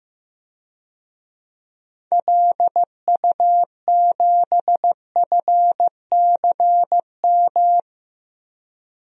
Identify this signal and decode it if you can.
Morse code, heard as a keyed tone, about 15 wpm, 700 Hz. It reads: LU7FCM
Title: DXCC-100 Call Signs - 6 Characters - Repeated 2x - 15wpm